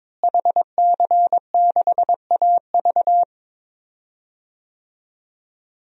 Morse code, heard as a keyed tone, about 22 wpm, 700 Hz. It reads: HC6AV
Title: DXCC-100 Call Signs - 5 Characters - Repeated 2x - 22wpm